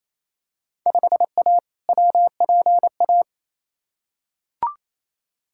Morse code, heard as a keyed tone, about 28 wpm, 700 Hz. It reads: 5A WPA
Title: Field Day Exchanges Encore - 28wpm